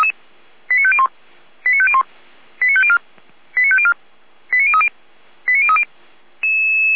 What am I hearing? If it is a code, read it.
Five-tone selective call (ZVEI), heard as quick runs of five tones, tones of about 70 ms. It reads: 89501 89501 89504 89504 89030 89030